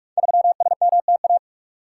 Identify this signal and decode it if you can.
Morse code, heard as a keyed tone, about 45 wpm, 700 Hz. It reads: 3SMTA
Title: Code Groups - 5 Letter:Numbers - 45wpm